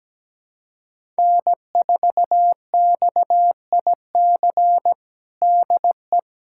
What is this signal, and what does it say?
Morse code, heard as a keyed tone, about 17 wpm, 700 Hz. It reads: N4XIC DE
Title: USA Callsign DE Callsign 17wpm